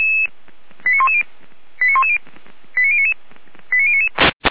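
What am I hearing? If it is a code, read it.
Five-tone selective call (ZVEI), heard as quick runs of five tones, tones of about 70 ms. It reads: 89110 89110 89099 89099
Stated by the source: ZVEI five-tone selective call received by SDR.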